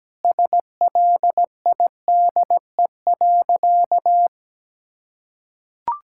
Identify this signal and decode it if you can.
Morse code, heard as a keyed tone, about 17 wpm, 700 Hz. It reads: SLIDE.
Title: Sentences from Top 1600 Words 17wpm